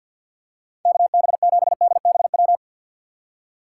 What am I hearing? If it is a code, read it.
Morse code, heard as a keyed tone, about 53 wpm, 700 Hz. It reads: KB7DBW
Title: USA Call Signs - 2x Character Spacing - 50wpm